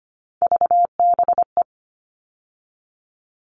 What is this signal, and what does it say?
Morse code, heard as a keyed tone, about 25 wpm, 700 Hz. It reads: VBE
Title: All Cut Numbers - 3-Digit 25wpm